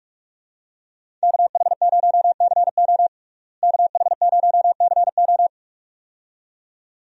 Morse code, heard as a keyed tone, about 47 wpm, 700 Hz. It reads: KH0CO KH0CO
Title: USA Call Signs - 5 Characters - Repeated 2x - 45wpm